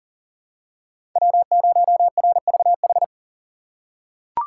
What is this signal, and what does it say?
Morse code, heard as a keyed tone, about 40 wpm, 700 Hz. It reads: W0RVH
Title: USA Call Signs - 2x Character Spacing - 40wpm